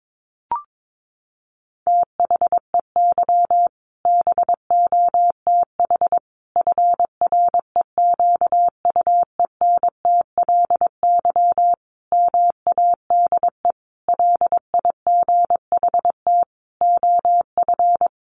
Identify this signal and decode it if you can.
Morse code, heard as a keyed tone, about 22 wpm, 700 Hz. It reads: THEY BOTH FREQUENTLY MADE LIGHT OF
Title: Sentences from Top 1300 Words 22wpm